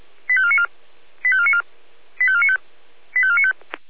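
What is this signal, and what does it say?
Five-tone selective call (ZVEI), heard as quick runs of five tones, tones of about 70 ms. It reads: 85494 85494 85495 85495
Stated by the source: Five-tone ZVEI selective call received by SDR.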